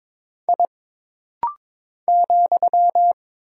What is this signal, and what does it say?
Morse code, heard as a keyed tone, about 22 wpm, 700 Hz. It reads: I ,
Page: This rapid-fire timing is geared to push your ICR (Instant Character Recognition) skills.